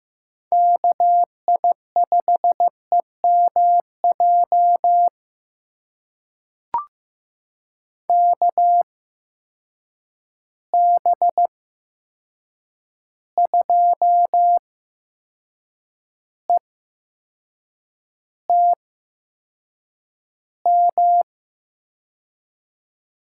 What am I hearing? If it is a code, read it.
Morse code, heard as a keyed tone, about 15 wpm, spaced as for 6 wpm, 700 Hz. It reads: KI5EMJ KB2ETM
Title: USA Call Signs - 6 Characters - 8x Character Spacing - 15wpm